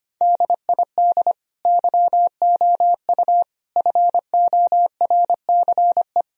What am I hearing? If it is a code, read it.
Morse code, heard as a keyed tone, about 25 wpm, 700 Hz. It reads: DID YOU FORCE
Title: Sentences from Top 1000 Words - Review Part 3 - 25wpm